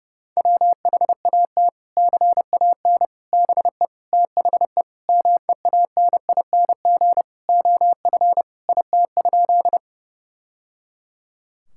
Morse code, heard as a keyed tone, about 30 wpm, 700 Hz. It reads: WHAT CAN BE THE MEANING OF IT?
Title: Sentences from Top 1000 Words - Review Part 6 - 30wpm